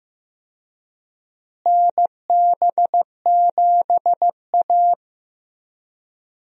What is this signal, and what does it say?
Morse code, heard as a keyed tone, about 15 wpm, 700 Hz. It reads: NB7A
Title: USA Call Signs - 4 Characters - Repeated 2x - 15wpm